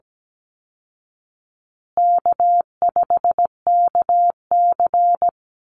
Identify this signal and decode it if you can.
Morse code, heard as a keyed tone, about 17 wpm, 700 Hz. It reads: K5KC
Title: Easy Call Signs Repeated 2x - 17wpm